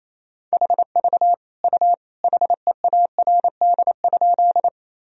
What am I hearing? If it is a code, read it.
Morse code, heard as a keyed tone, about 28 wpm, 700 Hz. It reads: HV U HEARD?
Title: Sentences from Top 400 Words and Abbreviations 28wpm